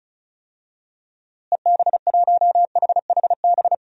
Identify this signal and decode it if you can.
Morse code, heard as a keyed tone, about 34 wpm, 700 Hz. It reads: EB1HHB
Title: DXCC-100 Call Signs - 6 Characters - Repeated 2x - 35wpm